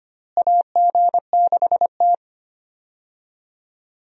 Morse code, heard as a keyed tone, about 25 wpm, 700 Hz. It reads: AG6T
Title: USA Call Signs - 4 Characters - Repeated 2x - 25wpm